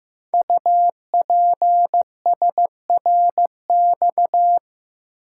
Morse code, heard as a keyed tone, about 15 wpm, 700 Hz. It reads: UPSRX
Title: Code Groups - 5 Letters - 15wpm